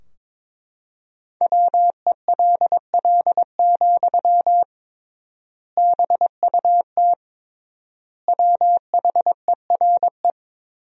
Morse code, heard as a keyed tone, about 22 wpm, 700 Hz. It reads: WELL, BUT WHERE
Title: 3x Word-Spacing - Sentences from Top 500 Words - 22wpm